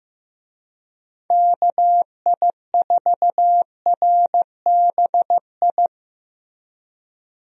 Morse code, heard as a keyed tone, about 15 wpm, 700 Hz. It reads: KI4RBI
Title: USA Call Signs - 6 Characters - Repeated 2x - 15wpm